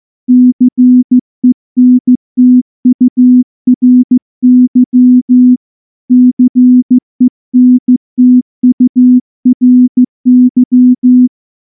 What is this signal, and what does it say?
Morse code, heard as a keyed tone, about 15 wpm, 250 Hz. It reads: CENTURY CENTURY
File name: morse-training-15WPM